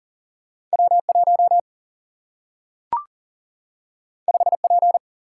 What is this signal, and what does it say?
Morse code, heard as a keyed tone, about 40 wpm, 700 Hz. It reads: W 1 5 P
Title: Single-Letter Single-Number 40wpm